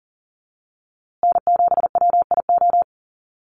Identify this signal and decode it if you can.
Morse code, heard as a keyed tone, about 39 wpm, 700 Hz. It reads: N7WIO
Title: USA Call Signs Repeated 2x 40wpm